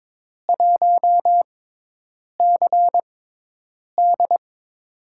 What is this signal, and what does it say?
Morse code, heard as a keyed tone, about 22 wpm, spaced as for 8 wpm, 700 Hz. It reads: 1CD
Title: USA Call Signs - 5 Characters - 6x Character Spacing - 22wpm